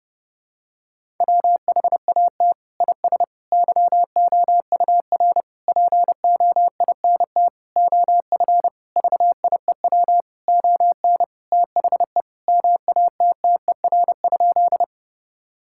Morse code, heard as a keyed tone, about 28 wpm, 700 Hz. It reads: WHAT IS YOUR POINT OF VIEW ON THE MATTER?
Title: Sentences from Top 1000 Words - Review Part 6 - 30wpm